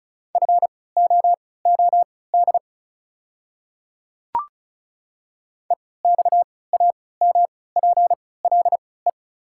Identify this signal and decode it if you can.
Morse code, heard as a keyed tone, about 36 wpm, spaced as for 20 wpm, 700 Hz. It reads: FOOD EXAMPLE
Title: 3x Character Spacing - Top 300 Words - 35wpm